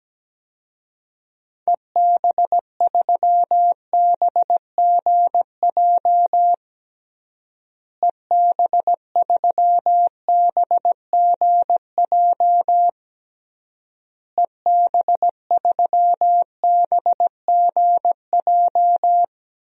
Morse code, heard as a keyed tone, about 17 wpm, 700 Hz. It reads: EB3BGJ EB3BGJ EB3BGJ
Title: DXCC-100 Call Signs - 6 Characters - Repeated 3x - 17wpm